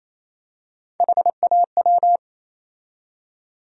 Morse code, heard as a keyed tone, about 28 wpm, 700 Hz. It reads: HAW
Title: USA Call Sign Postfixes - 3 Letters 28wpm